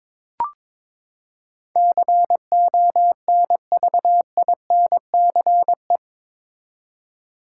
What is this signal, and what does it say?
Morse code, heard as a keyed tone, about 22 wpm, 700 Hz. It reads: CONVINCE
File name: practice / 22wpm / Eight Letter Words - 22wpm